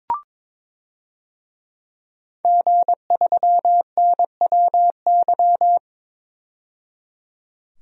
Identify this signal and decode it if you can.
Morse code, heard as a keyed tone, about 23 wpm, 700 Hz. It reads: G3NWY
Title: DXCC-100 Call Signs - 5 Characters - 22wpm